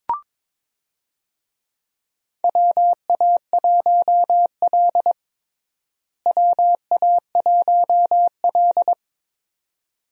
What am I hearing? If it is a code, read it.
Morse code, heard as a keyed tone, about 22 wpm, 700 Hz. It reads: WA1L WA1L
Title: USA Call Signs - 4 Characters - Repeated 2x - 22wpm